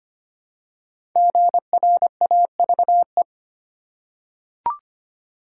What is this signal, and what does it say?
Morse code, heard as a keyed tone, about 25 wpm, 700 Hz. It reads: GRAVE
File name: 5 Letter Words - First 3 Letters Given - 25wpm